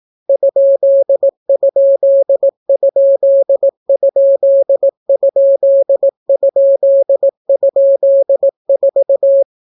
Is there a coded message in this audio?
Yes: Morse code from 0.3 to 9.4 s, heard as a keyed tone, about 18 wpm, 550 Hz. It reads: ???????4